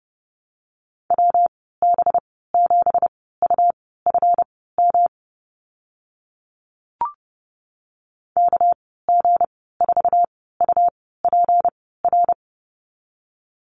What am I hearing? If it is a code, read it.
Morse code, heard as a keyed tone, about 30 wpm, spaced as for 17 wpm, 700 Hz. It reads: WB7UFM KG4UPR
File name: Lesson 37g - Letter x - Call Signs 30wpm